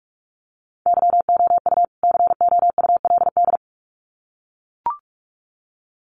Morse code, heard as a keyed tone, about 44 wpm, 700 Hz. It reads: YOU COULD
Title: Sets of 2 Words 45wpm